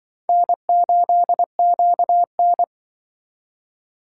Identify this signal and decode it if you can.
Morse code, heard as a keyed tone, about 25 wpm, 700 Hz. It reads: N8QN
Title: Speed Racing - USA Call Signs 20wpm